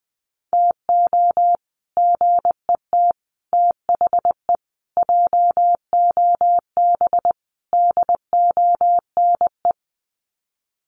Morse code, heard as a keyed tone, about 20 wpm, 700 Hz. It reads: TO GET THE JOB DONE
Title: Sets of 5 Words Encore2 20wpm